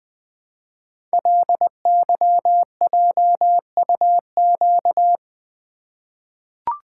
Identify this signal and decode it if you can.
Morse code, heard as a keyed tone, about 20 wpm, 700 Hz. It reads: LYJUQ
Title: Code Groups - 5 Letters - 20wpm